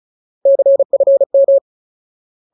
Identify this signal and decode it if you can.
Morse code, heard as a keyed tone, about 34 wpm, 550 Hz. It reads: CFM
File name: Ultimate Word List / 35WPM Ultimate